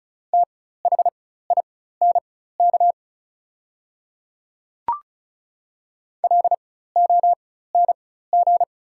Morse code, heard as a keyed tone, about 36 wpm, spaced as for 16 wpm, 690 Hz. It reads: THINK LONG
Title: 4x Character Spacing - Top 500 Words - 35wpm